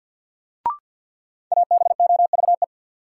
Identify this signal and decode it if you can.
Morse code, heard as a keyed tone, about 53 wpm, 700 Hz. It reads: ABOVE
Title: Sentences from Top 1200 Words - New Word Drill Alt - 50wpm